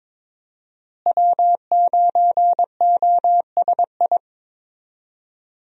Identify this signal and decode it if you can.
Morse code, heard as a keyed tone, about 21 wpm, 700 Hz. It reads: W9OSI
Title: USA Call Signs - 5 Characters - Repeated 3x - 22wpm